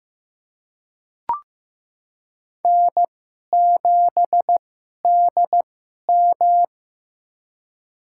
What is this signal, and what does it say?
Morse code, heard as a keyed tone, about 15 wpm, spaced as for 11 wpm, 700 Hz. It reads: N7DM
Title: USA Call Signs - 4 Characters - 2x Character Spacing - 15wpm